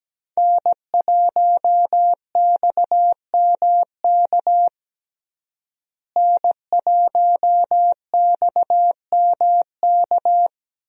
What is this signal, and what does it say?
Morse code, heard as a keyed tone, about 17 wpm, 700 Hz. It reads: N1XMK N1XMK
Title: USA Call Signs - 5 Characters - Repeated 2x - 17wpm